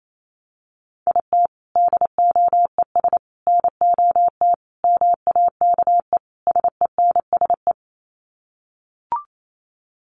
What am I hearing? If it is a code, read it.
Morse code, heard as a keyed tone, about 28 wpm, 700 Hz. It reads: IT DOES NOT MAKE SENSE
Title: Sets of 5 Words Encore2 28wpm